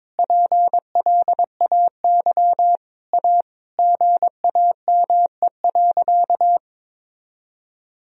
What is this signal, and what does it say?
Morse code, heard as a keyed tone, about 22 wpm, 700 Hz. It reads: PLAY A GAME.
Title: Sentences from Top 1000 Words - Review Part 4 - 22wpm